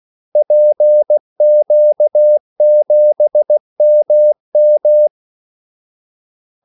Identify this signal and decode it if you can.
Morse code, heard as a keyed tone, about 16 wpm, 600 Hz. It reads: PQ7MM
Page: Kaldesignaler 16 wpm